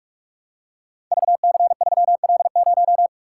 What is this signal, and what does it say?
Morse code, heard as a keyed tone, about 44 wpm, 700 Hz. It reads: VC3L0
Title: Code Groups - 5 Letter:Numbers - 45wpm